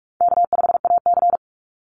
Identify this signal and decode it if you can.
Morse code, heard as a keyed tone, about 44 wpm, 700 Hz. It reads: K5AC
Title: USA Call Signs 45wpm